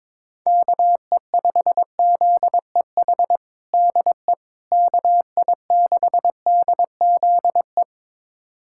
Morse code, heard as a keyed tone, about 22 wpm, 700 Hz. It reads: KE5ZEH DE KI6DZE
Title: USA Callsign DE Callsign 22wpm